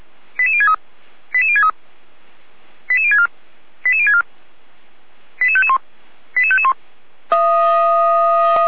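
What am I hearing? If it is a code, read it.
Five-tone selective call (ZVEI), heard as quick runs of five tones, tones of about 70 ms. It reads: 88063 88063 88064 88064 88551 88551
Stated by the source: ZVEI five-tone call received on SDR.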